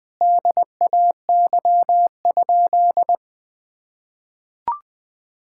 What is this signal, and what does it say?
Morse code, heard as a keyed tone, about 21 wpm, 700 Hz. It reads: DAY?
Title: Sentences from Top 1000 Words - Review Part 6 - 20wpm